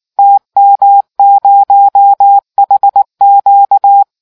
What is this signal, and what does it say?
Morse code, heard as a keyed tone, about 19 wpm, 800 Hz. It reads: TM0HQ